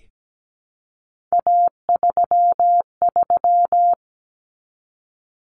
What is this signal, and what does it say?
Morse code, heard as a keyed tone, about 17 wpm, 700 Hz. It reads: A33
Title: Common Cut Numbers - 3-Digit 17wpm